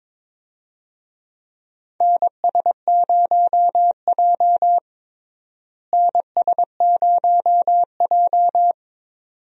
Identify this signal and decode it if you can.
Morse code, heard as a keyed tone, about 22 wpm, 700 Hz. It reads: NS0J NS0J
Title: USA Call Signs - 4 Characters - Repeated 2x - 22wpm